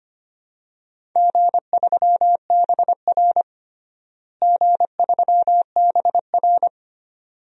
Morse code, heard as a keyed tone, about 25 wpm, 700 Hz. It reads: G3BR G3BR
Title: DXCC-100 Call Signs - 4 Characters - Repeated 2x - 25wpm